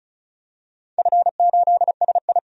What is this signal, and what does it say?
Morse code, heard as a keyed tone, about 34 wpm, 700 Hz. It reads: F8SI
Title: DXCC-100 Call Signs - 4 Characters - 35wpm